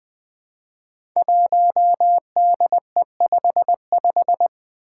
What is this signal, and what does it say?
Morse code, heard as a keyed tone, about 20 wpm, 700 Hz. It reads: 1DE55
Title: Five Challenging Letter-Number Combinations 20wpm